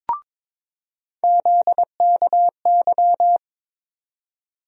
Morse code, heard as a keyed tone, about 22 wpm, 700 Hz. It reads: ZKY
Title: USA Call Sign Postfixes - 3 Letters 22wpm